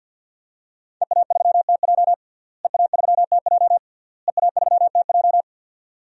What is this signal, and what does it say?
Morse code, heard as a keyed tone, about 51 wpm, 700 Hz. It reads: EA3TJ EA3TJ EA3TJ
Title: DXCC-100 Call Signs - 5 Characters - Repeated 3x - 50wpm